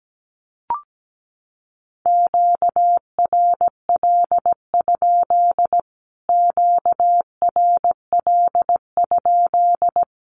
Morse code, heard as a keyed tone, about 17 wpm, 700 Hz. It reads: QRL? QRL?
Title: Q-Codes Repeated 3x 17wpm